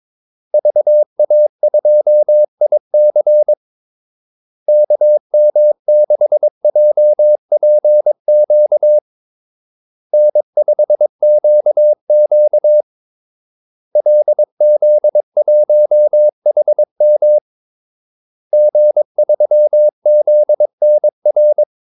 Morse code, heard as a keyed tone, about 22 wpm, 600 Hz. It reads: VA2IC KM6JPQ N5QQ LZ1HM G3ZNR
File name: Callsigns-22wpm